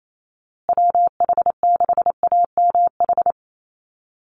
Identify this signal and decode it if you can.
Morse code, heard as a keyed tone, about 28 wpm, 700 Hz. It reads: WH6AMH
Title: USA Call Signs Repeated 3x 28wpm